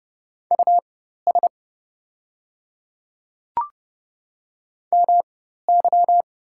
Morse code, heard as a keyed tone, about 30 wpm, spaced as for 14 wpm, 690 Hz. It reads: US MY
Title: 4x Character Spacing - Top 200 Words - 30wpm